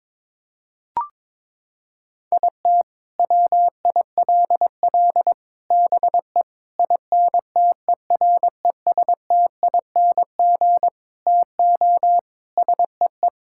Morse code, heard as a keyed tone, about 21 wpm, 700 Hz. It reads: IT WILL BE INTERESTING TO SEE
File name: practice / 22wpm / Sets of 8 Words - 22wpm